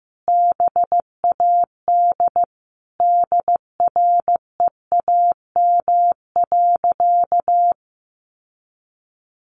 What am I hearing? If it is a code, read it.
Morse code, heard as a keyed tone, about 15 wpm, 700 Hz. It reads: BAD DREAM.
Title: Sentences from Top 1500 Words 15wpm